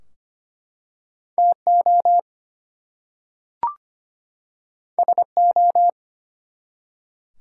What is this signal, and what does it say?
Morse code, heard as a keyed tone, about 25 wpm, 700 Hz. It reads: TO SO
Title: Two Letter Words - 25wpm